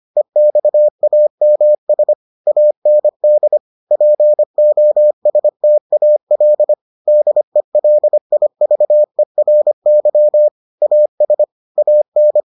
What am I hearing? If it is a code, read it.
Morse code, heard as a keyed tone, about 25 wpm, 600 Hz. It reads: EXAMS AND POSTAL DELIVERY AS AN